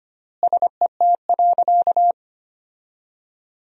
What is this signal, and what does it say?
Morse code, heard as a keyed tone, about 25 wpm, 700 Hz. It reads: SET.
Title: 3x Word-Spacing - Sentences from Top 400 Words - 25wpm